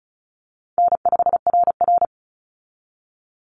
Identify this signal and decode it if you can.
Morse code, heard as a keyed tone, about 34 wpm, 700 Hz. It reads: N5RR
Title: Easy Call Signs Repeated 2x - 35wpm